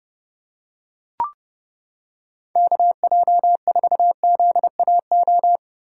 Morse code, heard as a keyed tone, about 30 wpm, 700 Hz. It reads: KJ4ZAO